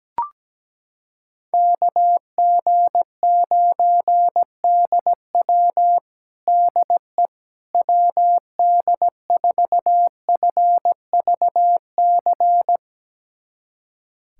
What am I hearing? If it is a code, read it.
Morse code, heard as a keyed tone, about 17 wpm, 700 Hz. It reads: KG9DW DE WD4FVC
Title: USA Callsign DE Callsign 17wpm